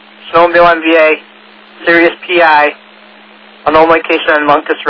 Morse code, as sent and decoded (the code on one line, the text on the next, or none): none